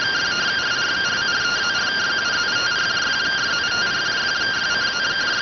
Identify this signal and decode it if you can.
Two-tone teleprinter signal, about 170 Hz shift.